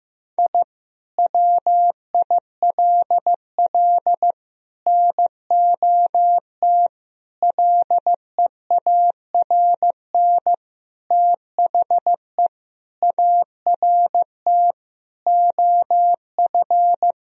Morse code, heard as a keyed tone, about 15 wpm, 700 Hz. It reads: I WILL NOT LEARN THE ART OF
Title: Sentences from Top 1000 Words - Review Part 3 - 15wpm